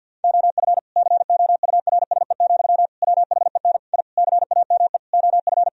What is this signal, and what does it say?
Morse code, heard as a keyed tone, about 50 wpm, 700 Hz. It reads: OF COURSE, WHEN I CAME OF
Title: Sentences from Top 1000 Words - Review Part 3 - 50wpm